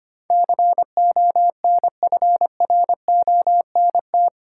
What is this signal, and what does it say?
Morse code, heard as a keyed tone, about 25 wpm, 690 Hz. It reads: CONFRONT
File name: 8 Letter Words - First 3 Letters Given - 25wpm